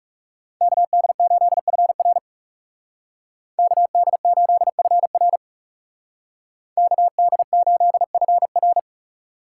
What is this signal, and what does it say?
Morse code, heard as a keyed tone, about 36 wpm, 700 Hz. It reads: KD8FR KD8FR KD8FR
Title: Speed Racing - USA 2x2 Call Signs 30wpm